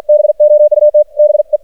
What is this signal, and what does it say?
Morse code, heard as a keyed tone, about 52 wpm, 600 Hz. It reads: BOUT BE
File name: archived / CRSnet